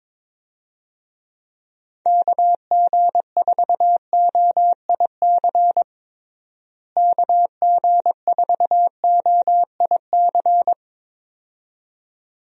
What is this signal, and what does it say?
Morse code, heard as a keyed tone, about 22 wpm, 700 Hz. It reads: KG4OIC KG4OIC
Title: USA Call Signs - 6 Characters - Repeated 2x - 22wpm